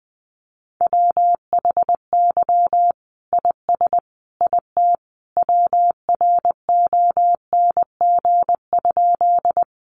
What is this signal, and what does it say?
Morse code, heard as a keyed tone, about 19 wpm, 700 Hz. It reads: WHY IS IT WRONG?
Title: Sentences from Top 900 Words 20wpm